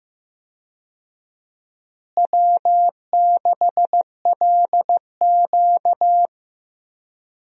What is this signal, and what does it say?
Morse code, heard as a keyed tone, about 15 wpm, 700 Hz. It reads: W6LQ
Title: USA Call Signs - 4 Characters - Repeated 3x - 15wpm